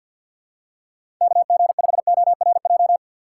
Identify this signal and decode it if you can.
Morse code, heard as a keyed tone, about 48 wpm, 690 Hz. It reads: KG5ORJ
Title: USA Call Signs - 3x Character Spacing - 50wpm